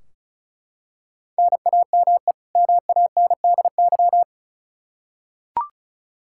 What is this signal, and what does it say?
Morse code, heard as a keyed tone, about 36 wpm, 700 Hz. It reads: NAME MANDY
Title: Practice QSO - Exchange 1 - Beginning - 35wpm